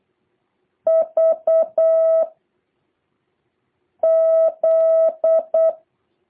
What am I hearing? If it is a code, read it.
Morse code, heard as a keyed tone, about 8 wpm, 650 Hz. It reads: V Z
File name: jh8cbh / 2023cw